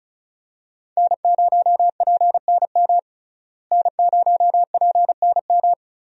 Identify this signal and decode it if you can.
Morse code, heard as a keyed tone, about 36 wpm, 700 Hz. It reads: N0PNM N0PNM
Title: USA Call Signs - 5 Characters - Repeated 2x - 35wpm